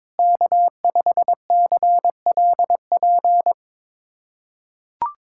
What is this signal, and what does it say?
Morse code, heard as a keyed tone, about 21 wpm, 700 Hz. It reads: K5CLP
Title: USA Callsign DE Callsign - 5 Characters - 22wpm